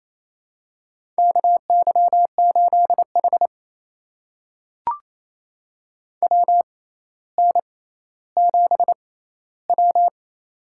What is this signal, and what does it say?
Morse code, heard as a keyed tone, about 28 wpm, spaced as for 14 wpm, 700 Hz. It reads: KY8H WN7W
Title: USA Call Signs - 4 Characters - 6x Character Spacing - 28wpm